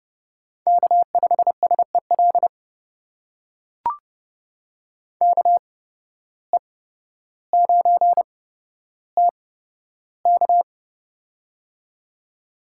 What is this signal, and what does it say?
Morse code, heard as a keyed tone, about 31 wpm, spaced as for 13 wpm, 700 Hz. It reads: K5SEL KE9TK
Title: USA Call Signs - 5 Characters - 8x Character Spacing - 30wpm